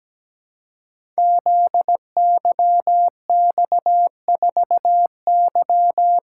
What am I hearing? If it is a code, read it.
Morse code, heard as a keyed tone, about 17 wpm, 700 Hz. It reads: ZYX4Y
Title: Code Groups - 5 Letter:Numbers - 17wpm